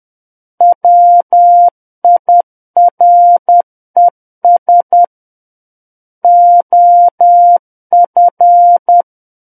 Morse code, heard as a keyed tone, about 10 wpm, 700 Hz. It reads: WIRES OF
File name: Book One / 01-Chapter 1 - 10 WPM